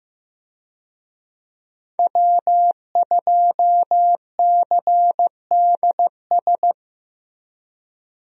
Morse code, heard as a keyed tone, about 15 wpm, 700 Hz. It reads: W2CDS
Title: USA Call Signs - 5 Characters - Repeated 3x - 15wpm